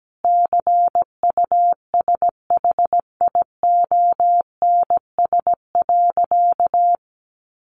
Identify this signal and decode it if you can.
Morse code, heard as a keyed tone, about 17 wpm, 700 Hz. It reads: CUSHIONS.
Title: A Sound Of Thunder 17wpm